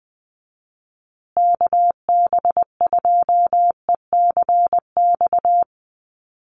Morse code, heard as a keyed tone, about 20 wpm, 700 Hz. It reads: KB2ECX
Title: USA Call Signs Repeated 3x 20wpm